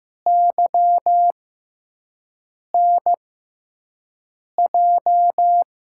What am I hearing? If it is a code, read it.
Morse code, heard as a keyed tone, about 15 wpm, spaced as for 5 wpm, 700 Hz. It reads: YNJ
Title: USA Call Signs - 6 Characters - 6x Character Spacing - 15wpm